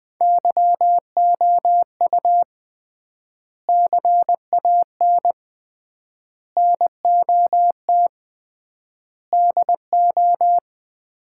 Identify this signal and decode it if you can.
Morse code, heard as a keyed tone, about 20 wpm, 700 Hz. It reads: YOU CAN NOT DO
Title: 3x Word-Spacing - Sets of 5 Words - 20wpm